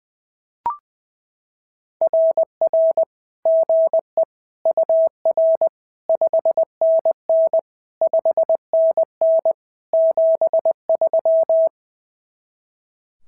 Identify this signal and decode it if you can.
Morse code, heard as a keyed tone, about 19 wpm, 650 Hz. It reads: RR GE UR 5NN 5NN 73
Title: SOTA QSO Practice 20wpm